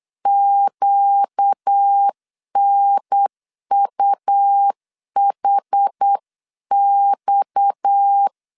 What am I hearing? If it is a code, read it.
Morse code, heard as a keyed tone, about 8 wpm, 800 Hz. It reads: QNUHX
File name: CW / DET / 1 6 1500C S C 8wpm 5219